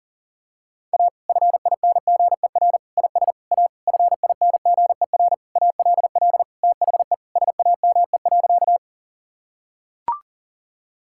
Morse code, heard as a keyed tone, about 40 wpm, 700 Hz. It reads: A FINGER IS A FINGER ALL THE SAME.
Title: Sentences from Top 1600 Words 40wpm